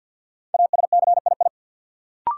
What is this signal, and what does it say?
Morse code, heard as a keyed tone, about 49 wpm, 700 Hz. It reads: ASCII
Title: Sets of 5 Words - New Word Drill - 50wpm